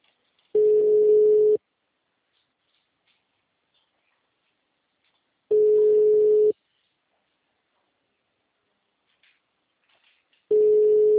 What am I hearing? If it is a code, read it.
Ringback tone (Europe, 425 Hz).
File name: out- -2006- - - .22600